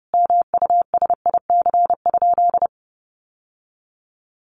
Morse code, heard as a keyed tone, about 30 wpm, 700 Hz. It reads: MUSIC?
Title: Sentences from Top 1000 Words 30wpm